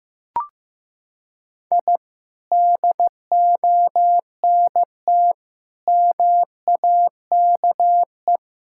Morse code, heard as a keyed tone, about 15 wpm, 700 Hz. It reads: I DONT MAKE
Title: Sentences from Top 1000 Words - Review Part 5 - 15wpm